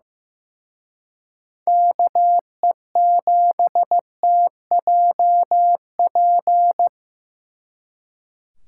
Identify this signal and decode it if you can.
Morse code, heard as a keyed tone, about 18 wpm, 700 Hz. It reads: KE7TJP
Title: Speed Racing - USA Call Signs 15wpm